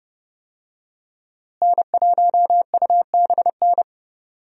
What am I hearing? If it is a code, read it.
Morse code, heard as a keyed tone, about 30 wpm, 700 Hz. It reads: N1UBN
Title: USA Call Signs - 5 Characters - Repeated 2x - 30wpm